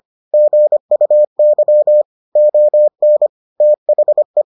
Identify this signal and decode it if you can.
Morse code, heard as a keyed tone, about 25 wpm, 600 Hz. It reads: GUY ON THE